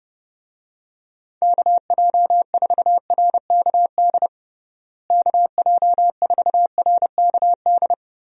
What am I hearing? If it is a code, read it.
Morse code, heard as a keyed tone, about 32 wpm, 700 Hz. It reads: KJ4RKD KJ4RKD
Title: USA Call Signs - 6 Characters - Repeated 3x - 30wpm